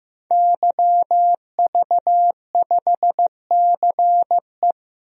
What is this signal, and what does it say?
Morse code, heard as a keyed tone, about 15 wpm, 700 Hz. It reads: YV5CE
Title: DXCC-100 Call Signs - 5 Characters - 15wpm